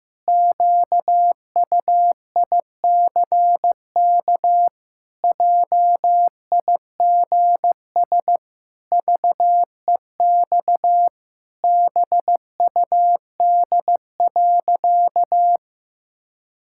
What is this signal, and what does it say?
Morse code, heard as a keyed tone, about 15 wpm, 700 Hz. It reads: QUICK JIGS VEX BUD.
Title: Pangram Challenge v2 - 15wpm